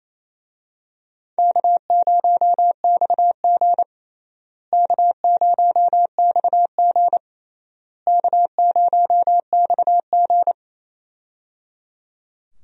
Morse code, heard as a keyed tone, about 28 wpm, 700 Hz. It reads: K0XG K0XG K0XG
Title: USA Call Signs - 4 Characters - Repeated 3x - 28wpm